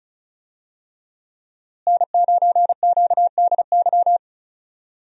Morse code, heard as a keyed tone, about 36 wpm, 700 Hz. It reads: N9QDY
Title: USA Call Signs - 5 Characters - Repeated 3x - 35wpm